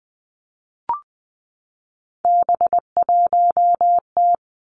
Morse code, heard as a keyed tone, about 19 wpm, 700 Hz. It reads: B1T
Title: Three Challenging Letter-Number Combinations 20wpm